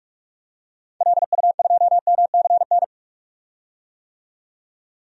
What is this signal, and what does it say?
Morse code, heard as a keyed tone, about 46 wpm, 700 Hz. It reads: LU2MCN
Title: DXCC-100 Call Signs - 6 Characters - 45wpm